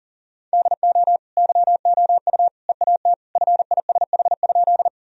Morse code, heard as a keyed tone, about 40 wpm, 700 Hz. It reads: DO YOU EAT FISH?
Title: Sentences from Top 1000 Words - Review Part 1 - 40wpm